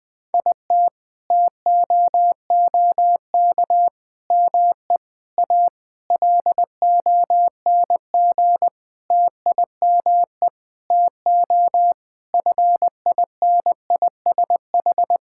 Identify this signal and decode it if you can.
Morse code, heard as a keyed tone, about 20 wpm, 700 Hz. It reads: IT TOOK ME A LONG TIME TO FINISH
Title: Sentences with 7-Word Phrases - 20wpm